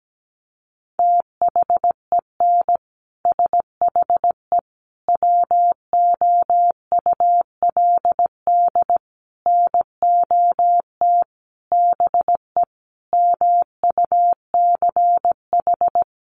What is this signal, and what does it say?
Morse code, heard as a keyed tone, about 17 wpm, 700 Hz. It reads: THEN SHE WOULD NOT BE MUCH
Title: Sentences from Top 1400 Words 17wpm